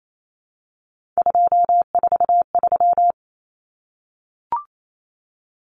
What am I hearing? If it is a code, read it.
Morse code, heard as a keyed tone, about 28 wpm, 700 Hz. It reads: 243
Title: Numbers - 3-Digit 28wpm